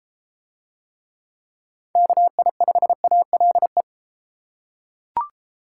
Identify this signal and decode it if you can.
Morse code, heard as a keyed tone, about 32 wpm, 700 Hz. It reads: KI5ALE